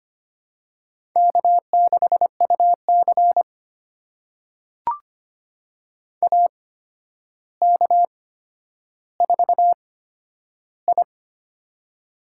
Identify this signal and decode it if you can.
Morse code, heard as a keyed tone, about 25 wpm, spaced as for 11 wpm, 700 Hz. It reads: K6UC AK4I
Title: USA Call Signs - 4 Characters - 8x Character Spacing - 25wpm